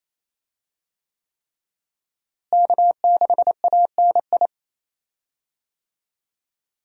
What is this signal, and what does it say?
Morse code, heard as a keyed tone, about 28 wpm, 700 Hz. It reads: K6ANI
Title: USA Call Signs - 5 Characters - Repeated 3x - 28wpm